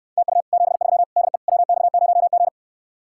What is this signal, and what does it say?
Morse code, heard as a keyed tone, about 69 wpm, 700 Hz. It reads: TU 73 DE WX0G